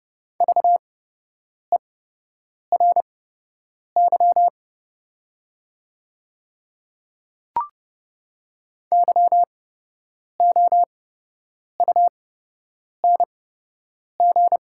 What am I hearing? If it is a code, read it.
Morse code, heard as a keyed tone, about 30 wpm, spaced as for 8 wpm, 700 Hz. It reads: VERY YOUNG
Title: 8x Character Spacing - Top 500 Words - 30wpm